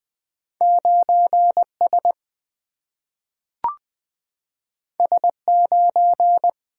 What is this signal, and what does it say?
Morse code, heard as a keyed tone, about 20 wpm, 700 Hz. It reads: 9S S9
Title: Single-Letter Single-Number 20wpm